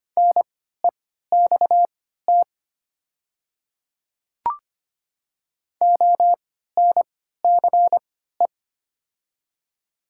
Morse code, heard as a keyed tone, about 25 wpm, spaced as for 14 wpm, 700 Hz. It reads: NEXT ONCE
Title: 3x Character Spacing - Top 500 Words - 25wpm